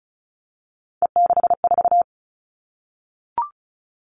Morse code, heard as a keyed tone, about 36 wpm, 700 Hz. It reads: E64
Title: Three Challenging Letter-Number Combinations 35wpm